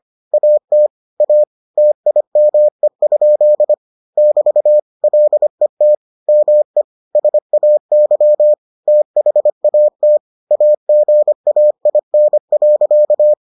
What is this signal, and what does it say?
Morse code, heard as a keyed tone, about 25 wpm, 600 Hz. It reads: AT A TIME? <BT> LET ME SAY THAT AGAIN.